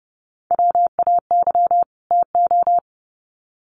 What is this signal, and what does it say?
Morse code, heard as a keyed tone, about 30 wpm, 700 Hz. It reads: WAY TO
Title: Sets of 2 Words 30wpm